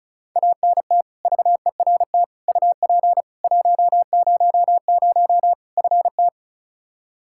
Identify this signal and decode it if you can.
Morse code, heard as a keyed tone, about 35 wpm, 700 Hz. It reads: ANT VERT UP 100 FT
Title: Practice QSO - Exchange 2 - Info - 35wpm